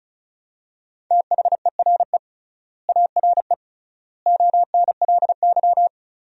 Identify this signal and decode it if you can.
Morse code, heard as a keyed tone, about 36 wpm, 700 Hz. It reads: THERE ARE ONLY
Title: 3x Word-Spacing - Sentences from Top 200 Words - 35wpm